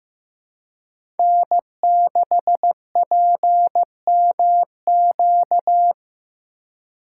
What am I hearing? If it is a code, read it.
Morse code, heard as a keyed tone, about 15 wpm, 700 Hz. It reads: N6PMQ
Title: USA Call Signs - 5 Characters - 6x Character Spacing - 15wpm